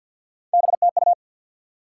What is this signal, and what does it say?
Morse code, heard as a keyed tone, about 50 wpm, 700 Hz. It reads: BTU
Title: Top 90 QSO Elements 50wpm